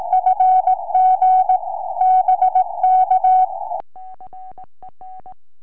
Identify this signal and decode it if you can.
Morse code, heard as a keyed tone, about 18 wpm, 750 Hz. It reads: FG BK
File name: sm4bdq 2m